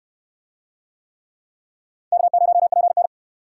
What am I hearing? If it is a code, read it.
Morse code, heard as a keyed tone, about 68 wpm, 700 Hz. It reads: K0PN